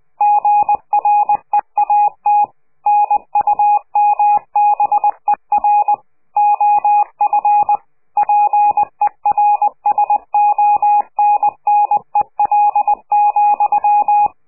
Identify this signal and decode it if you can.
Morse code, heard as a keyed tone, about 20 wpm, 830 Hz. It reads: GREAT NUMBER OF PERSONNEL,
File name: CQ de PCH laatste bericht farewell